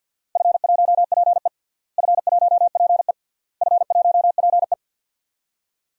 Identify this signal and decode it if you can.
Morse code, heard as a keyed tone, about 52 wpm, 700 Hz. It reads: F1PE F1PE F1PE
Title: DXCC-100 Call Signs - 4 Characters - Repeated 3x - 50wpm